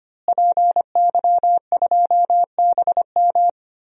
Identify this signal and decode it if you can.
Morse code, heard as a keyed tone, about 25 wpm, 700 Hz. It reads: PY2BM
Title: DXCC-100 Call Signs - 5 Characters - Repeated 3x - 25wpm